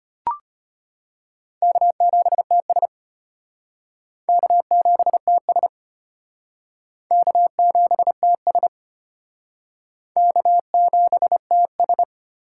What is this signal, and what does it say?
Morse code, heard as a keyed tone, about 32 wpm, 700 Hz. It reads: K7TS K7TS K7TS K7TS